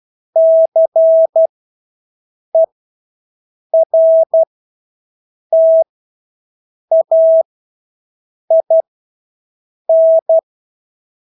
Morse code, heard as a keyed tone, about 12 wpm, spaced as for 6 wpm, 650 Hz. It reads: CERTAIN
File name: Engelske ord 12-6 wpm